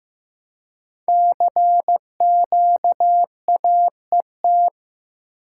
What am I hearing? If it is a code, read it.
Morse code, heard as a keyed tone, about 15 wpm, 700 Hz. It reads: CQAET
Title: Code Groups - 5 Letter:Numbers - 15wpm